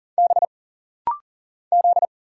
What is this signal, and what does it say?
Morse code, heard as a keyed tone, about 40 wpm, 700 Hz. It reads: B Z
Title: Single Letter-Number Alt - Rapid-Fire 40wpm